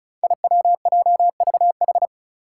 Morse code, heard as a keyed tone, about 36 wpm, 700 Hz. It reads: IWJVH
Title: Five Challenging Letter-Number Combinations 35wpm